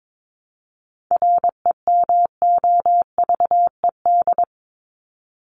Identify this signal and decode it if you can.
Morse code, heard as a keyed tone, about 22 wpm, 700 Hz. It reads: REMOVED
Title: ED Suffix - Top 100 Words with Base Words 22wpm